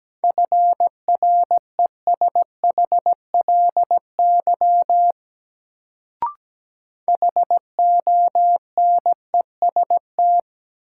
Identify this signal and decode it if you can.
Morse code, heard as a keyed tone, about 17 wpm, 700 Hz. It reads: FRESHLY HONEST
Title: Sets of 5 Words Encore - New Word Drill - 17wpm